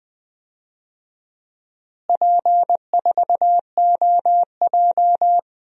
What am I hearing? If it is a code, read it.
Morse code, heard as a keyed tone, about 21 wpm, 700 Hz. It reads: P4OJ
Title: DXCC-100 Call Signs - 4 Characters - Repeated 3x - 20wpm